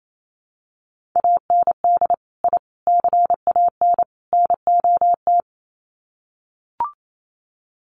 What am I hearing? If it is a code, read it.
Morse code, heard as a keyed tone, about 28 wpm, 700 Hz. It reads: AND I CAN NOT
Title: Sets of 4 Words 28wpm